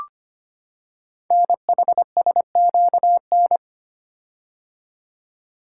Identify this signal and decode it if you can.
Morse code, heard as a keyed tone, about 25 wpm, 700 Hz. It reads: NHSQN
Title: Code Groups - 5 Letter:Numbers - 25wpm